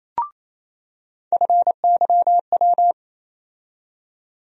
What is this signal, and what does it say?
Morse code, heard as a keyed tone, about 28 wpm, 700 Hz. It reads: FYW
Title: USA Call Sign Postfixes - 3 Letters 28wpm